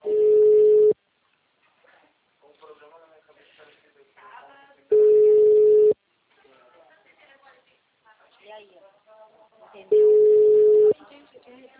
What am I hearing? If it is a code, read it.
Ringback tone (Europe, 425 Hz), with speech.
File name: out- -2001- - - .62208